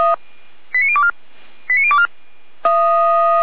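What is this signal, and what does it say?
Five-tone selective call (ZVEI), heard as quick runs of five tones, tones of about 70 ms. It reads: 89024 89024
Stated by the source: ZVEI five-tone selective call received by SDR, logged by Feuerwehr Orlamünde.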